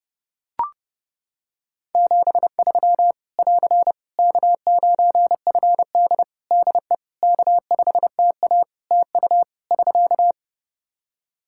Morse code, heard as a keyed tone, about 30 wpm, 700 Hz. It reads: 73 <AR> K9FD DE K5TA TU <SK>
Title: K1USN SST QSO Practice 30wpm